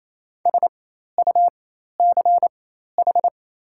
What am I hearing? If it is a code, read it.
Morse code, heard as a keyed tone, about 28 wpm, spaced as for 13 wpm, 690 Hz. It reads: SUCH
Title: 4x Character Spacing - Top 300 Words - 28wpm